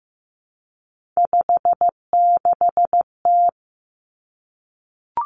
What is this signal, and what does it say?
Morse code, heard as a keyed tone, about 15 wpm, 700 Hz. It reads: 56T
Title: Three Challenging Letter-Number Combinations 15wpm